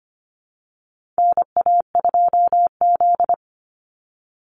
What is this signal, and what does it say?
Morse code, heard as a keyed tone, about 25 wpm, 700 Hz. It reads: NA2Z
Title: USA Call Signs 25wpm